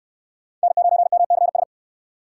Morse code, heard as a keyed tone, about 62 wpm, 700 Hz. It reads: N9MCS